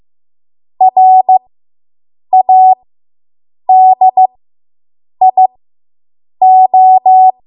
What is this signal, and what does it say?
Morse code, heard as a keyed tone, about 15 wpm, spaced as for 7 wpm, 750 Hz. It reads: RADIO